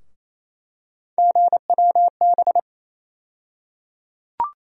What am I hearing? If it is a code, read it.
Morse code, heard as a keyed tone, about 28 wpm, 700 Hz. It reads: GWB
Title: USA Call Sign Postfixes - 3 Letters 28wpm